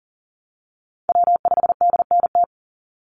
Morse code, heard as a keyed tone, about 40 wpm, 700 Hz. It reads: W5DNT
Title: Call Signs Repeated 2x - 40wpm